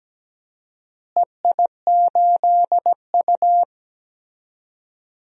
Morse code, heard as a keyed tone, about 17 wpm, 700 Hz. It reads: EI8U
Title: DXCC-100 Call Signs - 4 Characters - Repeated 2x - 17wpm